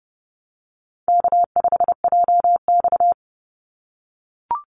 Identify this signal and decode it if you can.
Morse code, heard as a keyed tone, about 30 wpm, 700 Hz. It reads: K5JX
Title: Call Signs - 30wpm